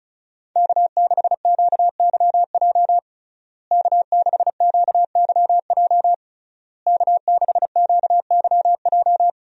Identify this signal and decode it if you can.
Morse code, heard as a keyed tone, about 36 wpm, 700 Hz. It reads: K6QYJ K6QYJ K6QYJ
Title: USA Call Signs - 5 Characters - Repeated 3x - 35wpm